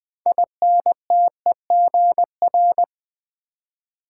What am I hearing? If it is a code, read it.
Morse code, heard as a keyed tone, about 20 wpm, 700 Hz. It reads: INTEGR
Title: Build A Word - 2 Letters At A Time - 8-Letter Words - 20wpm